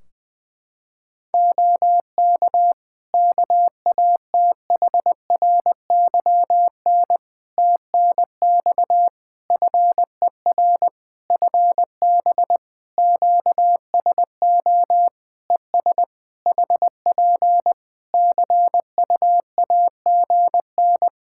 Morse code, heard as a keyed tone, about 21 wpm, 700 Hz. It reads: OK KATHRYN TNX FER FB QSO ES HP CUAGN
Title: K1USN SST QSO Practice 20wpm